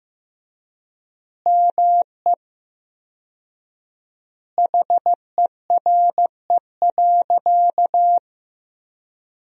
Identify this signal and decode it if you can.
Morse code, heard as a keyed tone, about 15 wpm, 700 Hz. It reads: ME HERE.
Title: 4x Word-Spacing - Sentences from Top 200 Words - 15wpm